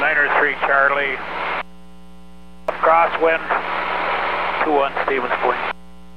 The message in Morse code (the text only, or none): none